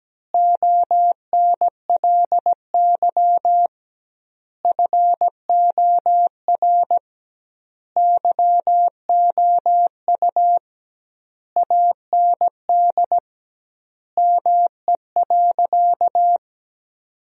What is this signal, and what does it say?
Morse code, heard as a keyed tone, about 17 wpm, 700 Hz. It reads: ONLY FOR YOU AND ME.
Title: 2x Word-Spacing - Sentences from Top 500 Words - 17wpm